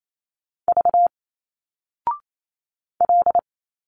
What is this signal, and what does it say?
Morse code, heard as a keyed tone, about 28 wpm, 700 Hz. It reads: V L
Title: Single Letters 28wpm